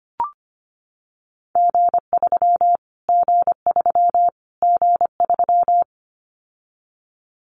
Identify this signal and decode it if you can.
Morse code, heard as a keyed tone, about 25 wpm, 700 Hz. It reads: G3 G3 G3
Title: Callsign Prefixes for Top 25 DXCC - Repeated 3x - 25wpm